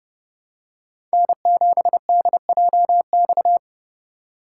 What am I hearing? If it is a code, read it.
Morse code, heard as a keyed tone, about 30 wpm, 700 Hz. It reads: N7DJX
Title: USA Call Signs - 5 Characters - 3x Character Spacing - 30wpm